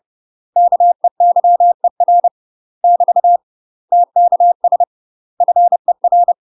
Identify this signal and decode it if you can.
Morse code, heard as a keyed tone, about 30 wpm, 700 Hz. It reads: KEYER <BT> TKS FER
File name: UK MASTER QSOs TRAINING / 30WPM / UK MASTER QSOs 0041